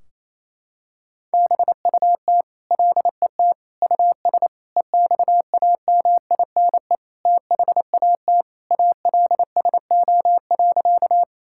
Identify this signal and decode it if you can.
Morse code, heard as a keyed tone, about 28 wpm, 700 Hz. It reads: BUT LET US EXAMINE THAT ALSO.
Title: Sentences from Top 1600 Words 28wpm